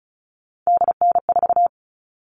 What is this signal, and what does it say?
Morse code, heard as a keyed tone, about 34 wpm, 700 Hz. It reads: DN4
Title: All Cut Numbers - 3-Digit 35wpm